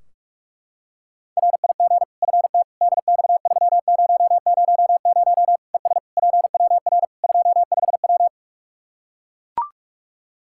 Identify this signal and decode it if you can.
Morse code, heard as a keyed tone, about 41 wpm, 700 Hz. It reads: RIG FT DX3000 ES PWR 25W
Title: K1USN SST QSO Practice 45wpm